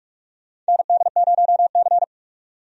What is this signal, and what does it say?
Morse code, heard as a keyed tone, about 44 wpm, 700 Hz. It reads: ND0C
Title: USA Call Signs - 4 Characters - 45wpm